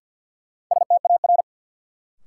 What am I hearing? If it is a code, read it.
Morse code, heard as a keyed tone, about 48 wpm, 700 Hz. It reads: STAR
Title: Sentences from Top 1000 Words - Review Part 2 - New Word Drill - 50wpm